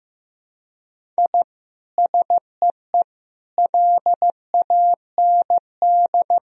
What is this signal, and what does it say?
Morse code, heard as a keyed tone, about 15 wpm, 700 Hz. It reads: I SEE LAND
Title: Sentences from Top 1000 Words - Review Part 1 - 15wpm